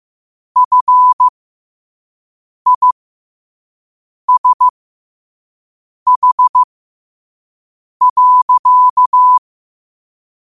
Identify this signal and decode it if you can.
Morse code, heard as a keyed tone, about 15 wpm, spaced as for 5 wpm, 1000 Hz. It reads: FISH.
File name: qotdfiles / 5wpm